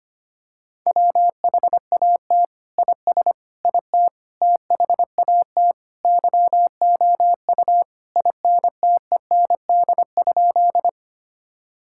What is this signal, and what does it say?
Morse code, heard as a keyed tone, about 25 wpm, 700 Hz. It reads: WHAT IS IT THAT YOU INTEND?
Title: Sentences from Top 1600 Words 25wpm